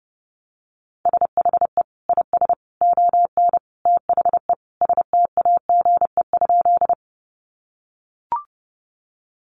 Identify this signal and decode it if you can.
Morse code, heard as a keyed tone, about 30 wpm, 700 Hz. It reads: SHE IS ON THE STAGE?
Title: Sentences from Top 1200 Words 30wpm